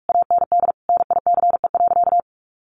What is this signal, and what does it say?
Morse code, heard as a keyed tone, about 44 wpm, 700 Hz. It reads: AND NICE.
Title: Sentences from Top 1200 Words 45wpm